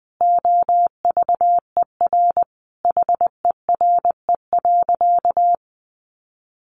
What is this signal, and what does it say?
Morse code, heard as a keyed tone, about 20 wpm, 700 Hz. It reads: OVER HERE.
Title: Sentences from Top 1500 Words 20wpm